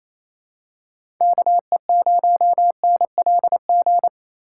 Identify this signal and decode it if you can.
Morse code, heard as a keyed tone, about 28 wpm, 700 Hz. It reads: KE0NLG
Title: USA Call Signs - 6 Characters - 3x Character Spacing - 28wpm